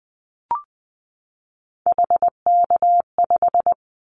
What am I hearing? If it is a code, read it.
Morse code, heard as a keyed tone, about 20 wpm, 700 Hz. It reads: HK5
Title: Callsign Prefixes for Top 100 DXCC - 20wpm